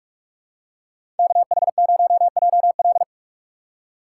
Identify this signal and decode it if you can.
Morse code, heard as a keyed tone, about 46 wpm, 700 Hz. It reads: KH0JL
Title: USA Call Signs - 5 Characters - 3x Character Spacing - 45wpm